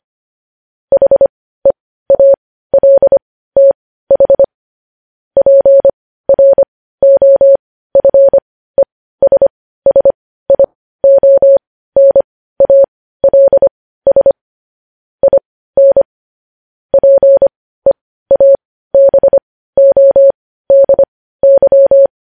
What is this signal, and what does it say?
Morse code, heard as a keyed tone, about 25 wpm, spaced as for 15 wpm, 550 Hz. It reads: HEALTH PROFESSIONALS IN PEABODY
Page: News Headlines in Morse Code at 15 WPM